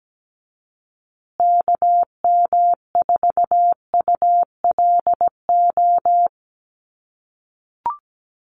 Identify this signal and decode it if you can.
Morse code, heard as a keyed tone, about 17 wpm, 700 Hz. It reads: KM4ULO
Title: USA Call Signs Repeated 3x 17wpm